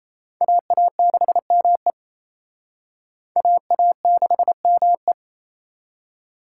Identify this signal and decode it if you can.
Morse code, heard as a keyed tone, about 35 wpm, 700 Hz. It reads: AA6ME AA6ME
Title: Speed Racing - USA 2x2 Call Signs 28wpm